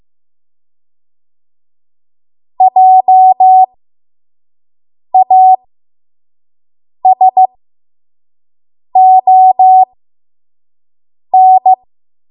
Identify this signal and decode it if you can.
Morse code, heard as a keyed tone, about 15 wpm, spaced as for 5 wpm, 750 Hz. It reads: JASON
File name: Morse / 05WPM